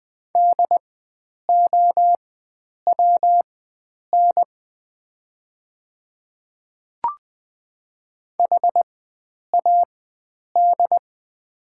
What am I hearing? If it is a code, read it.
Morse code, heard as a keyed tone, about 20 wpm, spaced as for 9 wpm, 700 Hz. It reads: DOWN HAD
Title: 4x Character Spacing - Top 100 Words - 20wpm